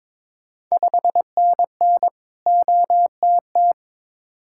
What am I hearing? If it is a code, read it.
Morse code, heard as a keyed tone, about 22 wpm, 690 Hz. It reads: 5NN OTT
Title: Ontario QSO Party Exchange 22wpm